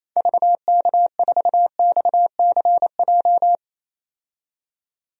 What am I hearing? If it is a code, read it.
Morse code, heard as a keyed tone, about 28 wpm, 700 Hz. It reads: VK4XCJ
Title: DXCC-100 Call Signs - 6 Characters - 28wpm